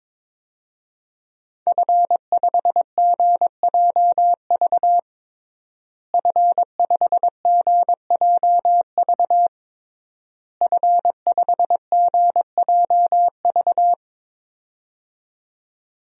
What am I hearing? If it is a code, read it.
Morse code, heard as a keyed tone, about 22 wpm, 700 Hz. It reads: F5GJV F5GJV F5GJV
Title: DXCC-100 Call Signs - 5 Characters - Repeated 3x - 22wpm